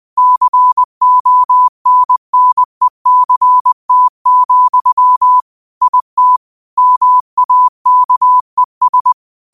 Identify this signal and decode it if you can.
Morse code, heard as a keyed tone, about 20 wpm, 1000 Hz. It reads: CONNECT, IT MAKES